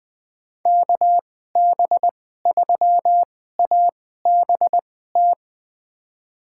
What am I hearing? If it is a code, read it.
Morse code, heard as a keyed tone, about 20 wpm, spaced as for 14 wpm, 700 Hz. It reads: KB3ABT
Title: USA Call Signs - 6 Characters - 2x Character Spacing - 20wpm